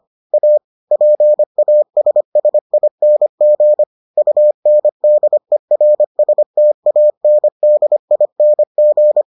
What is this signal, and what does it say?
Morse code, heard as a keyed tone, about 25 wpm, 600 Hz. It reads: A PASSING UNDERSTANDING